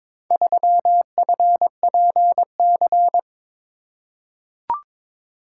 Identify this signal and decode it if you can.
Morse code, heard as a keyed tone, about 22 wpm, 700 Hz. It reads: 3FPC
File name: practice / 22wpm / Four Letter-Number Combinations - 22wpm